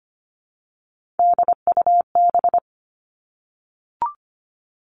Morse code, heard as a keyed tone, about 25 wpm, 700 Hz. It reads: DUB
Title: Three Challenging Letter-Number Combinations 25wpm